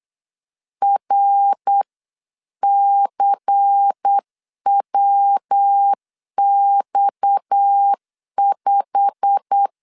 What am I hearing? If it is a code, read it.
Morse code, heard as a keyed tone, about 8 wpm, 800 Hz. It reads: RCWX5
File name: CW / DET / 1 9 1500C S C 8wpm 5653